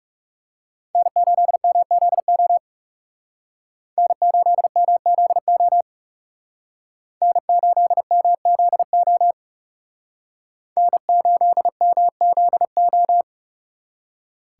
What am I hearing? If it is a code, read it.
Morse code, heard as a keyed tone, about 38 wpm, 700 Hz. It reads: N8MZO N8MZO N8MZO N8MZO